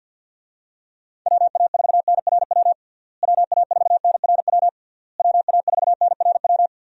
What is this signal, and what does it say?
Morse code, heard as a keyed tone, about 52 wpm, 700 Hz. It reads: WA4NRW WA4NRW WA4NRW
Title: USA Call Signs - 6 Characters - Repeated 3x - 50wpm